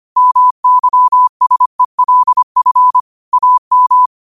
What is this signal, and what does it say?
Morse code, heard as a keyed tone, about 25 wpm, 1000 Hz. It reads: MYSELF AM
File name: qotdfiles / 25wpm